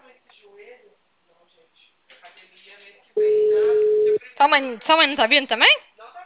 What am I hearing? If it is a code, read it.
Ringback tone (Europe, 425 Hz), with speech.
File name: out- -2004- - - .15410